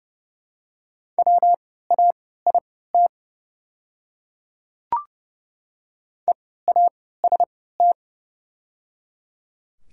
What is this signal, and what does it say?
Morse code, heard as a keyed tone, about 30 wpm, spaced as for 17 wpm, 690 Hz. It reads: WAIT EAST
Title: 3x Character Spacing - Top 500 Words - 30wpm